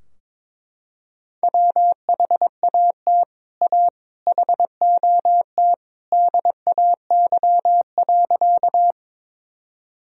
Morse code, heard as a keyed tone, about 22 wpm, 700 Hz. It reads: WHAT A HOT DAY.
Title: Sentences from Top 1000 Words - Review Part 3 - 22wpm